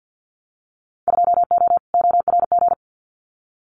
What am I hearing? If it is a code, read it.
Morse code, heard as a keyed tone, about 50 wpm, 700 Hz. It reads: 2O ORG
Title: Winter Field Day Exchanges 50wpm